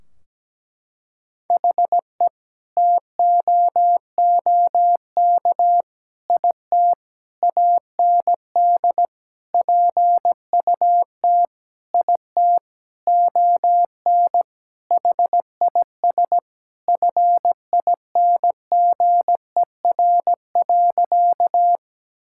Morse code, heard as a keyed tone, about 17 wpm, 700 Hz. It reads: HE TOOK IT AND PUT IT ON HIS FINGER.
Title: Sentences from Top 1600 Words 17wpm